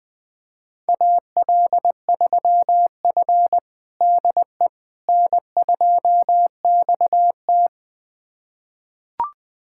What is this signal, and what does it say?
Morse code, heard as a keyed tone, about 20 wpm, 700 Hz. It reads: AL3F DE N2XT
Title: USA Callsign DE Callsign - 4 Characters - 20wpm